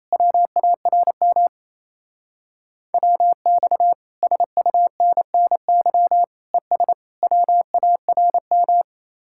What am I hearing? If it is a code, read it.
Morse code, heard as a keyed tone, about 34 wpm, 700 Hz. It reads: WARM WX SUNNY ES WARM
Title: Speed-Racing Common QSO Phrases - 28wpm